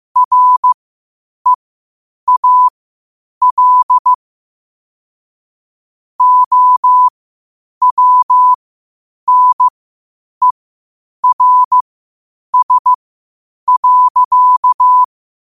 Morse code, heard as a keyed tone, about 15 wpm, spaced as for 8 wpm, 1000 Hz. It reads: REAL OWNERS.
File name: qotdfiles / 8wpm